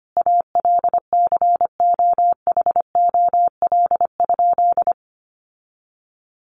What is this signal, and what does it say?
Morse code, heard as a keyed tone, about 25 wpm, 700 Hz. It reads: ALCOHOL?
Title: Sentences from Top 1400 Words 25wpm